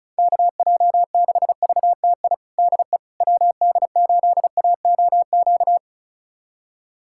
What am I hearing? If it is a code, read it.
Morse code, heard as a keyed tone, about 34 wpm, 700 Hz. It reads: KJ6VTI DE WD8AOQ
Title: USA Callsign DE Callsign - 6 Characters - 35wpm